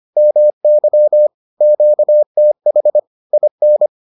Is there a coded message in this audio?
Yes: Morse code from 0.2 to 3.9 s, heard as a keyed tone, about 25 wpm, 600 Hz. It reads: MY QTH IN